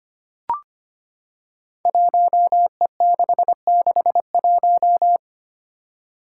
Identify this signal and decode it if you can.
Morse code, heard as a keyed tone, about 25 wpm, 700 Hz. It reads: 1E661
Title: Five Challenging Letter-Number Combinations 25wpm